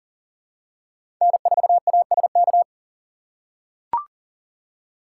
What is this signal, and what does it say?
Morse code, heard as a keyed tone, about 40 wpm, 700 Hz. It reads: N4ASK
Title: USA Call Signs - 3x Character Spacing - 40wpm